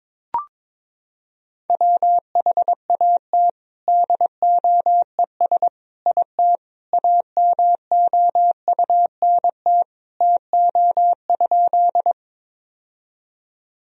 Morse code, heard as a keyed tone, about 22 wpm, 700 Hz. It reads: WHAT DOES IT AMOUNT TO?
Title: Sentences from Top 1000 Words - Review Part 4 - 22wpm